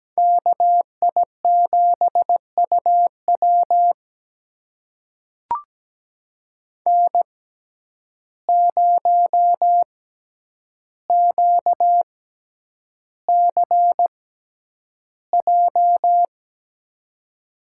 Morse code, heard as a keyed tone, about 17 wpm, spaced as for 9 wpm, 700 Hz. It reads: KI7UW N0QCJ
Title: USA Call Signs - 5 Characters - 6x Character Spacing - 17wpm